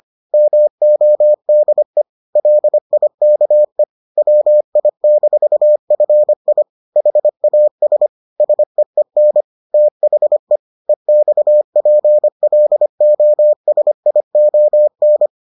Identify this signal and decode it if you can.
Morse code, heard as a keyed tone, about 25 wpm, 600 Hz. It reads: MODE LIKE WI-FI HAS SEEN THE EXPLOSION